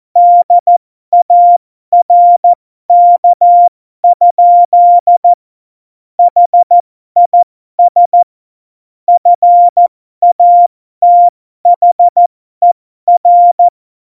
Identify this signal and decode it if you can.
Morse code, heard as a keyed tone, about 14 wpm, spaced as for 12 wpm, 700 Hz. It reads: DARK? HIS FATHER
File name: Morsekurs / Witze